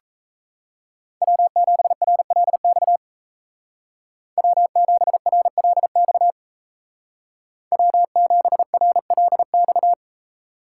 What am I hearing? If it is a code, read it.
Morse code, heard as a keyed tone, about 36 wpm, 700 Hz. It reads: W7RLX W7RLX W7RLX